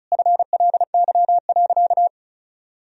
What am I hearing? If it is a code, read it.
Morse code, heard as a keyed tone, about 36 wpm, 700 Hz. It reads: FLY.
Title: Sentences from Top 1000 Words - Review Part 2 - 35wpm